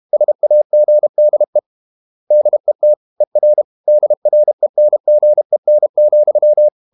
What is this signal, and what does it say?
Morse code, heard as a keyed tone, about 32 wpm, 600 Hz. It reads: SAGDE DET ER DRENGEN,
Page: Ferie 32 wpm